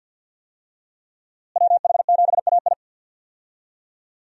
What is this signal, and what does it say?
Morse code, heard as a keyed tone, about 50 wpm, 700 Hz. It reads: WH7AI
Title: USA Call Signs - 5 Characters - Repeated 2x - 50wpm